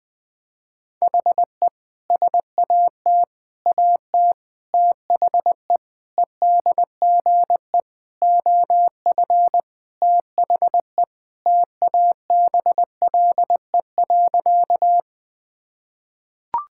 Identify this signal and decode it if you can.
Morse code, heard as a keyed tone, about 21 wpm, 700 Hz. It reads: HE SAT AT THE EDGE OF THE TABLE.
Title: Sentences with 3-Word Phrases - 20wpm